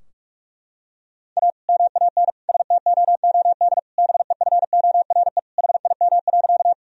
Morse code, heard as a keyed tone, about 46 wpm, 700 Hz. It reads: A MAN STOOD BEFORE HIM.
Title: Sentences from Top 1000 Words - Review Part 2 - 45wpm